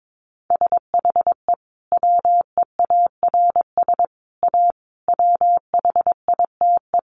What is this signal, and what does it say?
Morse code, heard as a keyed tone, about 22 wpm, 700 Hz. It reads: SHE WEARS A WHITE
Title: Additional Easy Phrases 22wpm